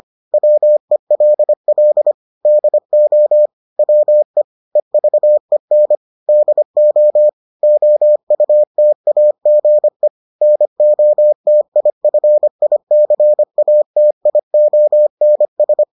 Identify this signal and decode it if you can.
Morse code, heard as a keyed tone, about 25 wpm, 600 Hz. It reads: WELL DO WE EVEN DO OUTAGE NOTIFICATIONS